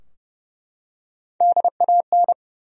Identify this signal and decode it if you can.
Morse code, heard as a keyed tone, about 30 wpm, 700 Hz. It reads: DAN
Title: USA Common Names - Max 4 Letters - 30wpm